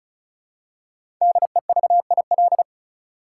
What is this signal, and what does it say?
Morse code, heard as a keyed tone, about 36 wpm, 700 Hz. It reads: DEVIL
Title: Five Letter Words - 35wpm